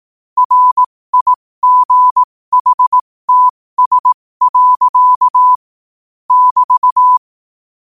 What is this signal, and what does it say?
Morse code, heard as a keyed tone, about 18 wpm, spaced as for 15 wpm, 1000 Hz. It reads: RIGHTS. <BT>
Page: New quotes every day in morse code at 15 Words per minute.